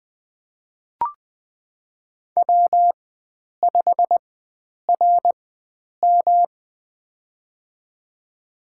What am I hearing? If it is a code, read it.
Morse code, heard as a keyed tone, about 20 wpm, spaced as for 9 wpm, 700 Hz. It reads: W5RM
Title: USA Call Signs - 4 Characters - 4x Character Spacing - 20wpm